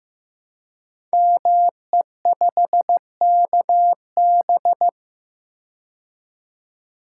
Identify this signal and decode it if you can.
Morse code, heard as a keyed tone, about 15 wpm, 700 Hz. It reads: ME5KB
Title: Code Groups - 5 Letter:Numbers - 15wpm